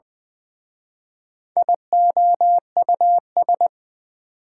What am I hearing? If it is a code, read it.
Morse code, heard as a keyed tone, about 24 wpm, 700 Hz. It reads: IOUS
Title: Speed-Racing Prefixes and Suffixes 20wpm